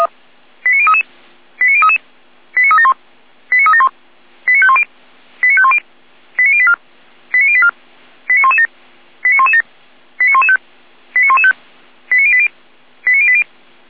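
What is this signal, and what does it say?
Five-tone selective call (ZVEI), heard as quick runs of five tones, tones of about 70 ms. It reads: 89033 89033 89371 89371 89510 89510 89084 89084 89117 89117 89115 89115 89080 89080
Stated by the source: ZVEI five-tone selective call received by SDR.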